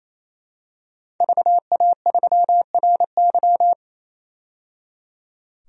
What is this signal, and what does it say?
Morse code, heard as a keyed tone, about 27 wpm, 700 Hz. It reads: VA3RY
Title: DXCC-100 Call Signs - 5 Characters - 28wpm